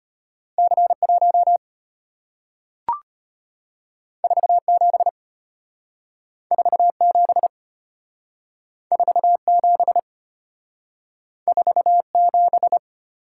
Morse code, heard as a keyed tone, about 34 wpm, 690 Hz. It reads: C1 47 47 47 47